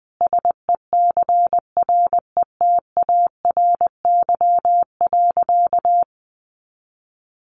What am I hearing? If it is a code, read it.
Morse code, heard as a keyed tone, about 20 wpm, 700 Hz. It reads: SECRETARY.
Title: Sentences from Top 1300 Words 20wpm